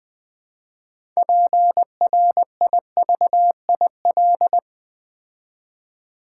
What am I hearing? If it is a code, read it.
Morse code, heard as a keyed tone, about 20 wpm, 700 Hz. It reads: PRIVIL
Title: Build A Word - 3 Letters At A Time - 9-Letter Words - 20wpm